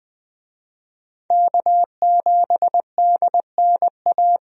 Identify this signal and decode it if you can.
Morse code, heard as a keyed tone, about 20 wpm, 700 Hz. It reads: K7DNA
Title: USA Call Signs - 5 Characters - 20wpm